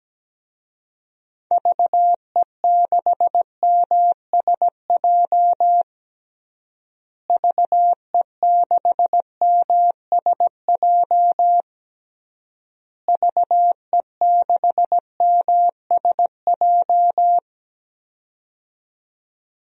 Morse code, heard as a keyed tone, about 17 wpm, 700 Hz. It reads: VE6MSJ VE6MSJ VE6MSJ
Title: DXCC-100 Call Signs - 6 Characters - Repeated 3x - 17wpm